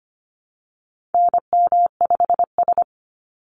Morse code, heard as a keyed tone, about 25 wpm, 700 Hz. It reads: NM5S
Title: NA SOTA Chaser Callsigns 25wpm